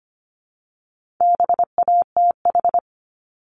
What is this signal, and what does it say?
Morse code, heard as a keyed tone, about 25 wpm, 700 Hz. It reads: BATH
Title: Multi-letter Phonemes Repeated 3x 25wpm